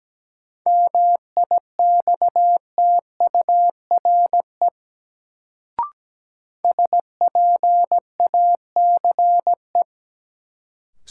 Morse code, heard as a keyed tone, about 17 wpm, 700 Hz. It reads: MIXTURE SPACE
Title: Sentences from Top 1500 Words - New Word Drill Alt - 17wpm